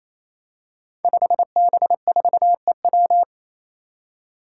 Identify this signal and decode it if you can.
Morse code, heard as a keyed tone, about 28 wpm, 700 Hz. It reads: 5B4EW
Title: Five Challenging Letter-Number Combinations 28wpm